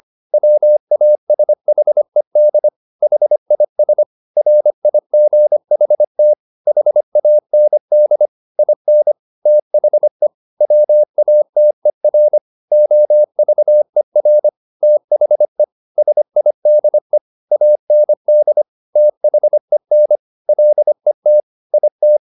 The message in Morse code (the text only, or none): WASHED HIS RIGHT HAND IN THE WATER OVER THE SIDE AND THEN LET IT